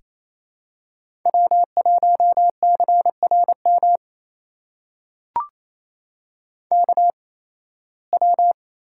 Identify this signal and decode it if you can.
Morse code, heard as a keyed tone, about 28 wpm, spaced as for 18 wpm, 700 Hz. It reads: W1CRM KW
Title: USA Call Signs - 5 Characters - 8x Character Spacing - 28wpm